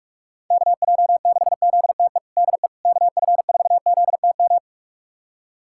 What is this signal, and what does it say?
Morse code, heard as a keyed tone, about 46 wpm, 700 Hz. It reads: KJ6ZTE DE KF4ZTM
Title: USA Callsign DE Callsign - 6 Characters - 45wpm